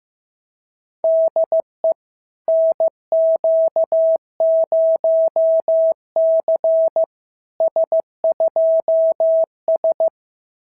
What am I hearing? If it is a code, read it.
Morse code, heard as a keyed tone, about 15 wpm, 650 Hz. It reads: DE NQ0C S2S
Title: SOTA QSO Practice 15wpm